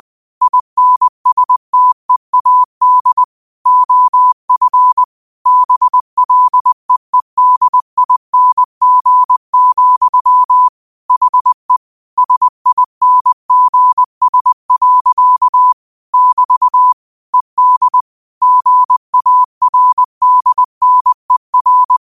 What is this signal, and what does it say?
Morse code, heard as a keyed tone, about 20 wpm, 1000 Hz. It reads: INSTEAD OF BLEEDING, HE SINGS. <BT> ED GARDNER
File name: qotdfiles / 20wpm